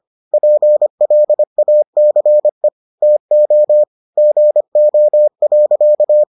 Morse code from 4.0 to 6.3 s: GO.

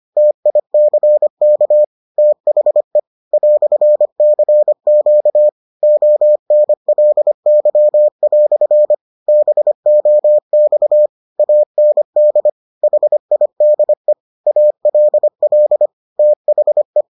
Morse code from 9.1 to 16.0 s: BOX AND HIDE ALL

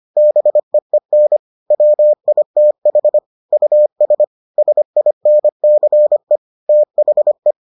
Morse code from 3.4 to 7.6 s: US SINCE THE

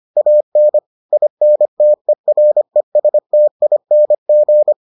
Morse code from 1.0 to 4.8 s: INTERESTING

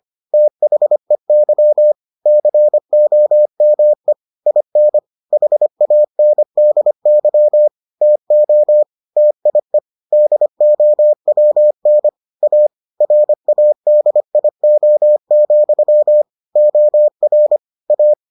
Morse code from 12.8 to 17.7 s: RADIO, OR

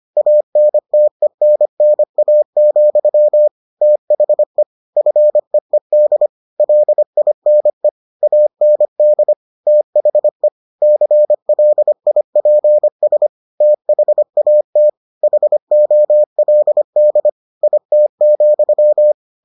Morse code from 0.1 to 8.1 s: ANTENNA, THE FEED LINE

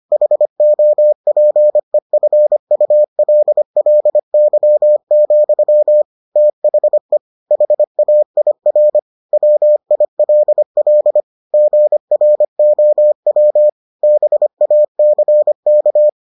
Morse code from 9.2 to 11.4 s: WILL

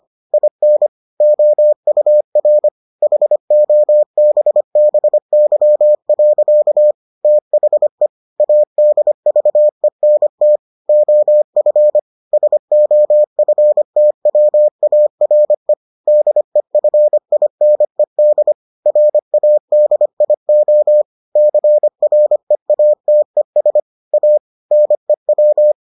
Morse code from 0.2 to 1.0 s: IN

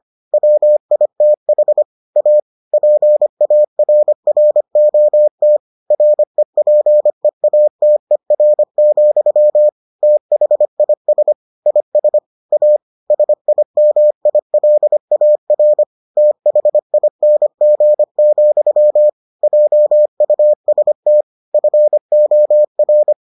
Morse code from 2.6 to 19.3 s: PARROT REPEATER, THIS IS A SIMILAR THING,

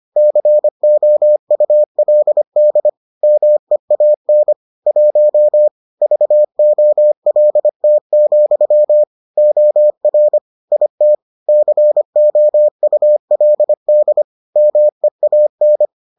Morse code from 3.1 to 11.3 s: MEAN 1 VOLT, OR IT

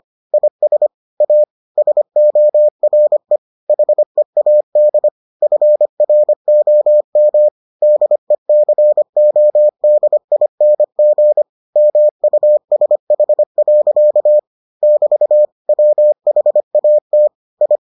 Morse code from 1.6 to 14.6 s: SORE HEAD FROM DECODING MUSH.

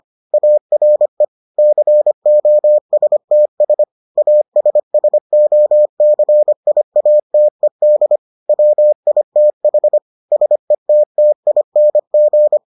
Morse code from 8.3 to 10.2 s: WITH

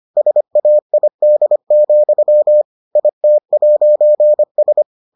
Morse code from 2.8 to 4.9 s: IT'S

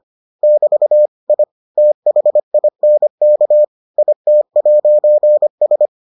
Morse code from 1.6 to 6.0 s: THINK IT'S

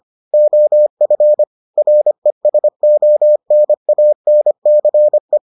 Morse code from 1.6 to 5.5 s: RESONANCE

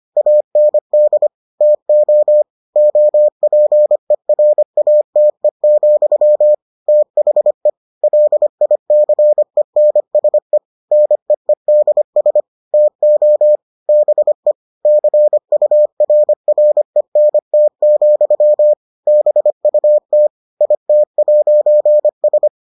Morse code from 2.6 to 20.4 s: OPERATE, THE LICENSE NEEDS TO BE CURRENT, BUT

